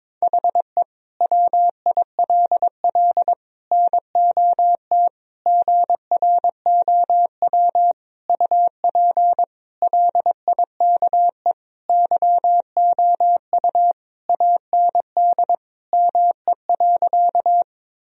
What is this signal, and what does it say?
Morse code, heard as a keyed tone, about 22 wpm, 700 Hz. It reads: HE WILL NOT GROW UP LIKE YOU AND ME.
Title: Sentences from Top 1000 Words - Review Part 1 - 22wpm